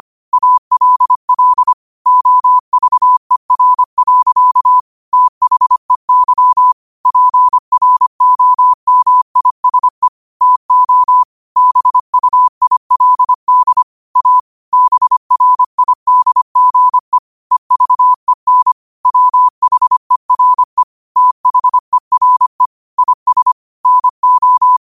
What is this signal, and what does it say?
Morse code, heard as a keyed tone, about 25 wpm, 1000 Hz. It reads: ALL OVER. THEY PROMISE TO BUILD A BRIDGE EVEN WHERE THERE IS NO
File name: qotdfiles / 25wpm